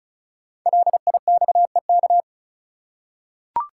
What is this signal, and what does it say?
Morse code, heard as a keyed tone, about 36 wpm, 700 Hz. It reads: LIXEK
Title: Code Groups - 5 Letter:Numbers - 35wpm